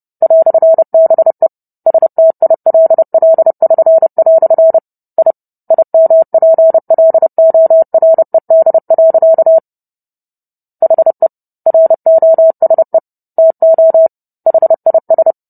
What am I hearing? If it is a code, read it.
Morse code, heard as a keyed tone, about 30 wpm, 650 Hz. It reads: "BE STILL<SN>" I IMPLORED. HE ROSE TO HIS